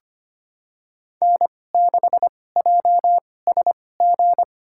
Morse code, heard as a keyed tone, about 25 wpm, spaced as for 18 wpm, 700 Hz. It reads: N6JSG
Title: USA Call Signs - 2x Character Spacing - 25wpm